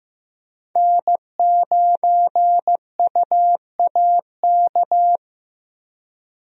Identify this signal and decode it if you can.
Morse code, heard as a keyed tone, about 15 wpm, 700 Hz. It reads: N9UAK
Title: USA Call Signs - 5 Characters - Repeated 2x - 15wpm